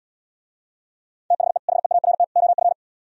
Morse code, heard as a keyed tone, about 77 wpm, 700 Hz. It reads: THE HEART OF